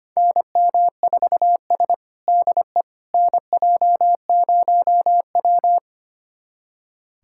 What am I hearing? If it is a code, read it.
Morse code, heard as a keyed tone, about 25 wpm, 700 Hz. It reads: NM4S DE NJ0W
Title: USA Callsign DE Callsign - 4 Characters - 25wpm